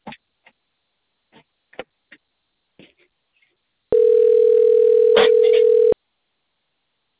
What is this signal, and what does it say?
Ringback tone (North America), 440 + 480 Hz.